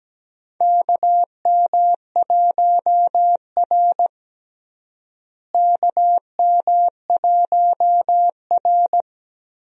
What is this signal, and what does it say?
Morse code, heard as a keyed tone, about 17 wpm, 700 Hz. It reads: KM1R KM1R
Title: USA Call Signs - 4 Characters - Repeated 2x - 17wpm